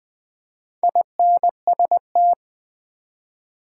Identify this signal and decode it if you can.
Morse code, heard as a keyed tone, about 20 wpm, 700 Hz. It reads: INST
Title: Build A Word - 2 Letters At A Time - 8-Letter Words - 20wpm